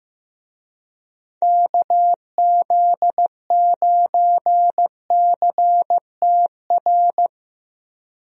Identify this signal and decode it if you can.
Morse code, heard as a keyed tone, about 15 wpm, 700 Hz. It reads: KZ9CTR
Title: USA Call Signs - 6 Characters - 15wpm